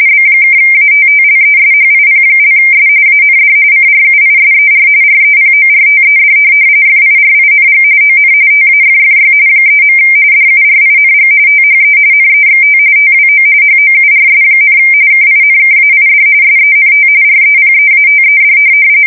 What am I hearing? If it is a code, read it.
RTTY, about 45.45 baud. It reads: B(ATTLESHIP)S AND CRUISER REPORTED FIRING ON TAFFY 3. DURING THE NEXT TWO HOURS, KINKAID SENT REPEATED
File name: fsk170 / USS-New-Jersey